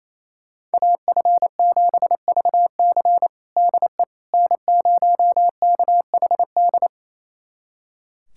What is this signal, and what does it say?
Morse code, heard as a keyed tone, about 28 wpm, 700 Hz. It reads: AF7VC DE N0KHD
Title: USA Callsign DE Callsign - 5 Characters - 28wpm